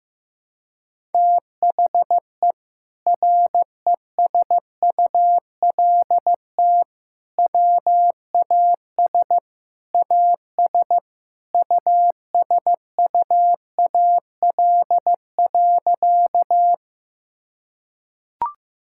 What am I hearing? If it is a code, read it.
Morse code, heard as a keyed tone, about 16 wpm, 700 Hz. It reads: THE RESULT WAS AS USUAL.
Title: Sentences from Top 1000 Words - Review Part 5 - 15wpm